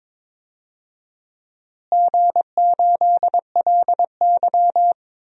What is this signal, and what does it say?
Morse code, heard as a keyed tone, about 22 wpm, 700 Hz. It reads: G8LY
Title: DXCC-100 Call Signs - 4 Characters - Repeated 2x - 22wpm